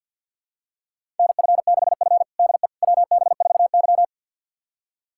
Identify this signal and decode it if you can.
Morse code, heard as a keyed tone, about 51 wpm, 700 Hz. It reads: NF6F DE WB4Y
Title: USA Callsign DE Callsign - 4 Characters - 50wpm